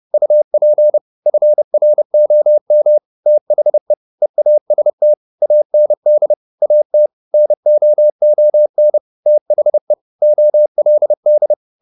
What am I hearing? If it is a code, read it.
Morse code, heard as a keyed tone, about 30 wpm, 600 Hz. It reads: UP FROM THE EAST AND AT NOON THE OLD
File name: the old man and the sea / the old man and the sea 1 30wpm